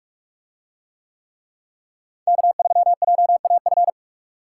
Morse code, heard as a keyed tone, about 45 wpm, 700 Hz. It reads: K3JAF